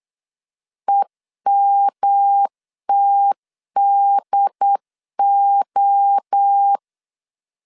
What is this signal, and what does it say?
Morse code, heard as a keyed tone, about 8 wpm, 800 Hz. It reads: EMTDO